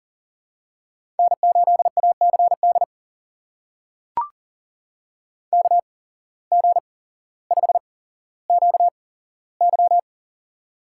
Morse code, heard as a keyed tone, about 40 wpm, spaced as for 17 wpm, 700 Hz. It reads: N8ACD KG5QY
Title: USA Call Signs - 5 Characters - 8x Character Spacing - 40wpm